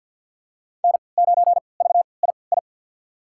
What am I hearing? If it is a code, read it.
Morse code, heard as a keyed tone, about 48 wpm, spaced as for 28 wpm, 700 Hz. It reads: N9VII
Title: USA Call Signs - 5 Characters - 3x Character Spacing - 50wpm